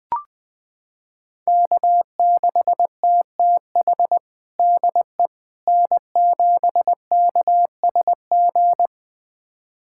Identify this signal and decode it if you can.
Morse code, heard as a keyed tone, about 20 wpm, 700 Hz. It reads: K6TTH DE N7KSG
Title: USA Callsign DE Callsign - 5 Characters - 20wpm